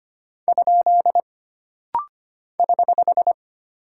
Morse code, heard as a keyed tone, about 25 wpm, 700 Hz. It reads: ? <HH>
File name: Single Punctuation-Prosign - Rapid-Fire - 25wpm